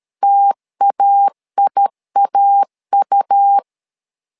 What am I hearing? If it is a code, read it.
Morse code, heard as a keyed tone, about 13 wpm, 800 Hz. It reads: TAIAU
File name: LEZIONI / 2 UHDN / 2 U 12wpm 1338